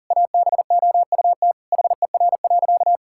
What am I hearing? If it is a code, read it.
Morse code, heard as a keyed tone, about 40 wpm, 700 Hz. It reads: ABOUT HER.
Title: Sentences from Top 1600 Words 40wpm